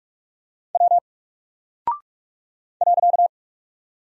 Morse code, heard as a keyed tone, about 44 wpm, 700 Hz. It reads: W .
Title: Single Character (Letter:Number:Punctuation:Prosign) Alt 45wpm